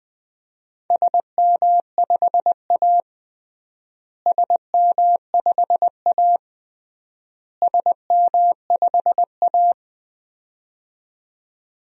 Morse code, heard as a keyed tone, about 20 wpm, 700 Hz. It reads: SM5A SM5A SM5A
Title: DXCC-100 Call Signs - 4 Characters - Repeated 3x - 20wpm